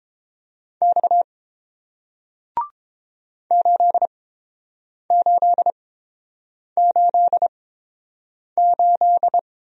Morse code, heard as a keyed tone, about 29 wpm, spaced as for 7 wpm, 700 Hz. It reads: X 8888